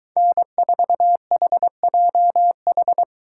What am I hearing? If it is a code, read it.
Morse code, heard as a keyed tone, about 22 wpm, 700 Hz. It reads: N4HJH